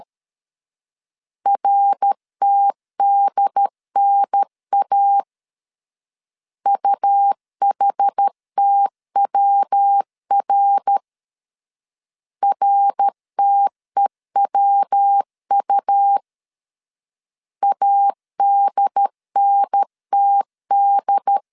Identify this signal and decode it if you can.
Morse code, heard as a keyed tone, about 13 wpm, 800 Hz. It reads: RTDNA UHTWR RTEWU ADNTD
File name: CW / DET / 1 3 1500C S C 12wpm 2750